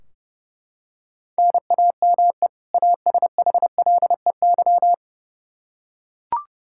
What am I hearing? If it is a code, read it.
Morse code, heard as a keyed tone, about 29 wpm, 700 Hz. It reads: NAME ASHLEY
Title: Practice QSO - Exchange 1 - Beginning - 30wpm